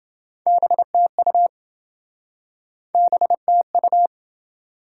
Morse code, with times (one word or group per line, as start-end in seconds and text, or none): 0.5-1.5 BTU
3.0-4.1 BTU